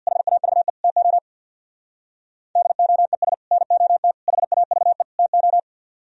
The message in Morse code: HAVE TO DOES NOT HAVE TO

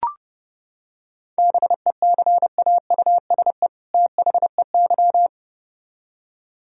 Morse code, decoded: BECAUSE THEY

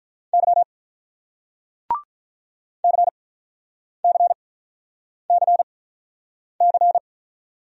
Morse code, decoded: Y CCCC

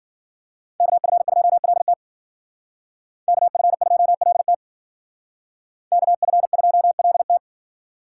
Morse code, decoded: KF2LT KF2LT KF2LT

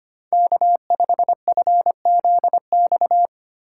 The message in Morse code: K5FZX

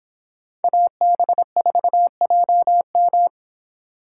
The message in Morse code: AB4JM

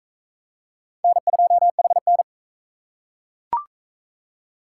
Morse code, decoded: N2HN

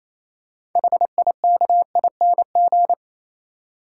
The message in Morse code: HIKING

